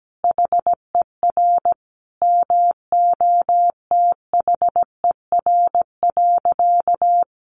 Morse code, heard as a keyed tone, about 17 wpm, 700 Hz. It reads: HER MOTHER.
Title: Sentences from Top 1500 Words 17wpm